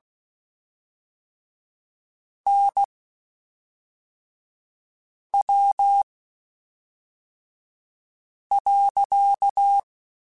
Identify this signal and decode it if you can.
Morse code, heard as a keyed tone, about 16 wpm, spaced as for 3 wpm, 800 Hz. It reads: NW.
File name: Spaced Characters / Spaced Characters-mp3s / Weather5 12 sp @5